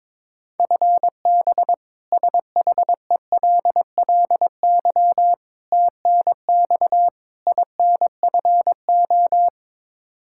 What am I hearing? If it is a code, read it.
Morse code, heard as a keyed tone, about 21 wpm, 700 Hz. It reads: FB SHELLY TNX INFO
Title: K1USN SST QSO Practice 22wpm